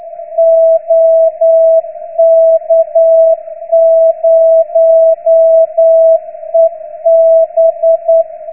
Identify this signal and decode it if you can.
Morse code, heard as a keyed tone, about 9 wpm, 660 Hz. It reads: OK0EB